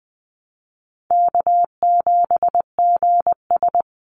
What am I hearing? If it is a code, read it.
Morse code, heard as a keyed tone, about 20 wpm, 700 Hz. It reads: K7GS
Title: Easy Call Signs Repeated 3x - 20wpm